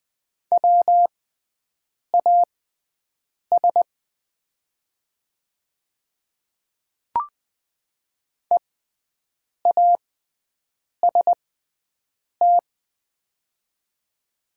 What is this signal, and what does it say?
Morse code, heard as a keyed tone, about 20 wpm, spaced as for 7 wpm, 700 Hz. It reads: WAS EAST
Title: 6x Character Spacing - Top 500 Words - 20wpm